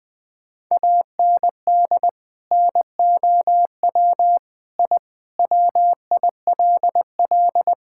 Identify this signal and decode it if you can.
Morse code, heard as a keyed tone, about 21 wpm, 700 Hz. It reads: AND NOW I WILL
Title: Sentences from Top 1000 Words - Review Part 6 - 20wpm